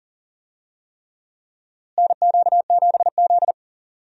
Morse code, heard as a keyed tone, about 40 wpm, 700 Hz. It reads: NQ7Z
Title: USA Call Signs - 4 Characters - Repeated 3x - 40wpm